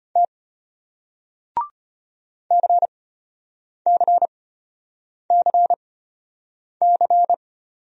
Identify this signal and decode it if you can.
Morse code, heard as a keyed tone, about 34 wpm, spaced as for 8 wpm, 700 Hz. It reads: T CCCC